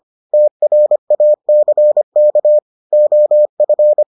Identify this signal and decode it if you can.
Morse code, heard as a keyed tone, about 25 wpm, 600 Hz. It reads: TRACK OF